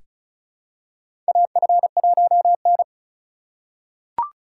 Morse code, heard as a keyed tone, about 36 wpm, 700 Hz. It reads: AF1N
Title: USA Call Signs - 4 Characters - 3x Character Spacing - 35wpm